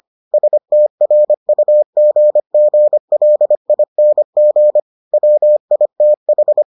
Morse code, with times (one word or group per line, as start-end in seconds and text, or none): 0.3-4.8 STRUGGLING
5.1-6.6 WITH